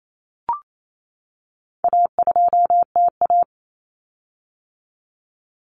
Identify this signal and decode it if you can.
Morse code, heard as a keyed tone, about 28 wpm, 700 Hz. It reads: A2TA
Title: Common Cut Numbers - 4-Digit 28wpm